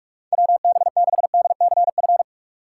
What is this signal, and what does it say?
Morse code, heard as a keyed tone, about 47 wpm, 700 Hz. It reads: WB6DCF
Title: USA Call Signs - 6 Characters - 3x Character Spacing - 45wpm